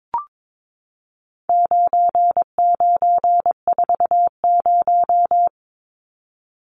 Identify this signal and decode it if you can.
Morse code, heard as a keyed tone, about 22 wpm, 700 Hz. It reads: 9940
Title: Numbers - 4-Digit 22wpm